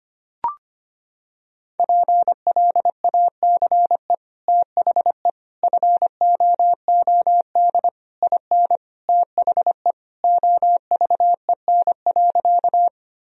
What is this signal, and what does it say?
Morse code, heard as a keyed tone, about 25 wpm, 700 Hz. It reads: PLACE THE FOOD IN THE OVEN.
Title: Sentences from Top 1000 Words - Review Part 4 - 25wpm